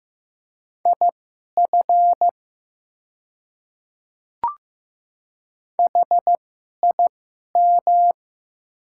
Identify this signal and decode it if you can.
Morse code, heard as a keyed tone, about 15 wpm, spaced as for 11 wpm, 700 Hz. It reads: IF HIM
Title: 2x Character Spacing - Top 100 Words - 15wpm